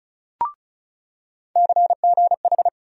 Morse code, heard as a keyed tone, about 34 wpm, 700 Hz. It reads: CGH
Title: USA Call Sign Postfixes - 3 Letters 35wpm